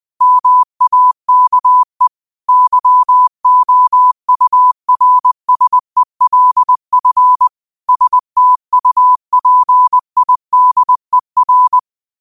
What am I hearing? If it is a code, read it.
Morse code, heard as a keyed tone, about 20 wpm, 1000 Hz. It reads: MAKE YOURSELF STUPIDER
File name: 20wpm